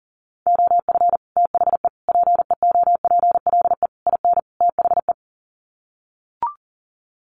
Morse code, heard as a keyed tone, about 40 wpm, 700 Hz. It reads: OF THE PEOPLE IN THE
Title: Sets of 5 Words Encore 40wpm